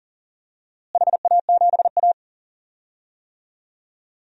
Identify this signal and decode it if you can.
Morse code, heard as a keyed tone, about 40 wpm, 700 Hz. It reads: HA7A
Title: DXCC-100 Call Signs - 4 Characters - Repeated 2x - 40wpm